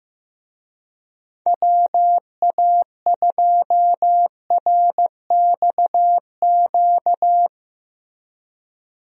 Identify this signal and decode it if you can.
Morse code, heard as a keyed tone, about 16 wpm, 700 Hz. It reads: WA2RXQ
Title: USA Call Signs - 6 Characters - Repeated 2x - 15wpm